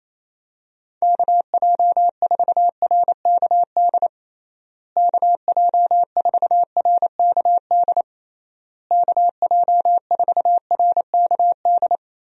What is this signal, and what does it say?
Morse code, heard as a keyed tone, about 27 wpm, 700 Hz. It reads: KJ4RKD KJ4RKD KJ4RKD
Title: USA Call Signs - 6 Characters - Repeated 3x - 28wpm